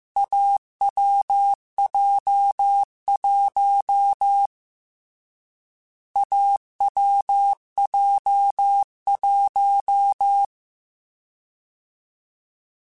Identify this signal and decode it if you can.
Morse code, heard as a keyed tone, about 15 wpm, 800 Hz. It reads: AWJ1 AWJ1